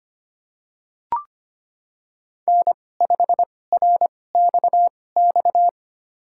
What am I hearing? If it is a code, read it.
Morse code, heard as a keyed tone, about 25 wpm, spaced as for 18 wpm, 700 Hz. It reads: N5RXX
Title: USA Call Signs - 5 Characters - 2x Character Spacing - 25wpm